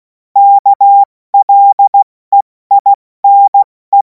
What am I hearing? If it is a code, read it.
Morse code, heard as a keyed tone, about 16 wpm, 800 Hz. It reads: KLEINE
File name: Morsekurs / Witze / Witz 03. März 2026 70 F80